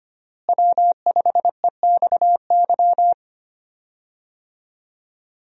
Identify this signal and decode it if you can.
Morse code, heard as a keyed tone, about 25 wpm, 700 Hz. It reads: W5EXY
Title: USA Call Signs - 5 Characters - Repeated 2x - 25wpm